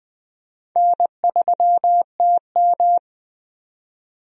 Morse code, heard as a keyed tone, about 20 wpm, 700 Hz. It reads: N3TM
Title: USA Call Signs - 4 Characters - 20wpm